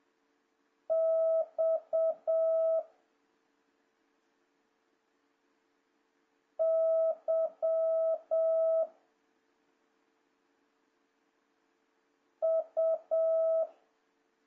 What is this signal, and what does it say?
Morse code, heard as a keyed tone, about 7 wpm, 650 Hz. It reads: X Y U
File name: jh8cbh / 2026cw